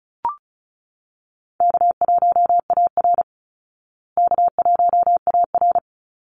Morse code, heard as a keyed tone, about 34 wpm, 700 Hz. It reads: K1AR K1AR
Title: Easy Call Signs Repeated 2x - 35wpm